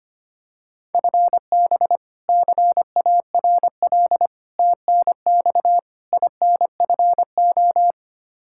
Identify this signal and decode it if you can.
Morse code, heard as a keyed tone, about 25 wpm, 700 Hz. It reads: FB CARL TNX INFO
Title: Practice QSO - Exchange 3 - More Info - 25wpm